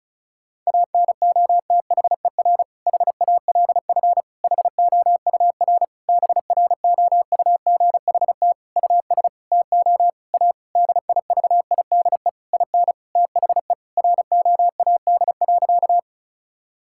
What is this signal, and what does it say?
Morse code, heard as a keyed tone, about 36 wpm, 700 Hz. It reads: ANOTHER HALF HOUR BROUGHT US TO A DIVIDE IN THE ROAD.
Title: Sentences from Top 1600 Words 35wpm